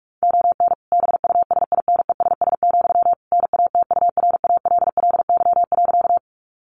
Morse code, heard as a keyed tone, about 44 wpm, 700 Hz. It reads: ON BUSINESS, NATURALLY.
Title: Sentences from Top 1300 Words 45wpm